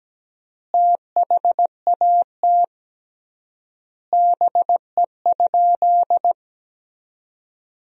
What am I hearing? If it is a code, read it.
Morse code, heard as a keyed tone, about 17 wpm, 700 Hz. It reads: THAT BE?
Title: 3x Word-Spacing - Sentences from Top 100 Words - 17wpm